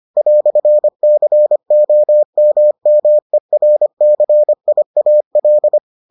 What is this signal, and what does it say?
Morse code, heard as a keyed tone, about 25 wpm, 600 Hz. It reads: "COMMERCIAL